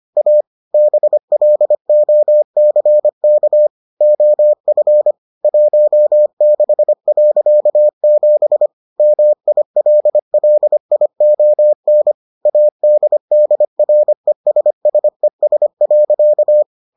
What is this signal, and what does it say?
Morse code, heard as a keyed tone, about 25 wpm, 600 Hz. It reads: A BLOCK OF 16.7 MILLION ADDRESSES.